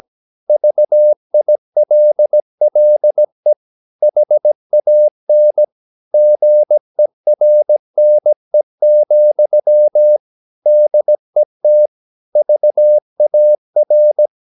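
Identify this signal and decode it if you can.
Morse code, heard as a keyed tone, about 17 wpm, 600 Hz. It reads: VILLE HAN GERNE, DET VAR
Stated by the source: Slow (QRS) Morse practice at 17 wpm, from CW med Gnister.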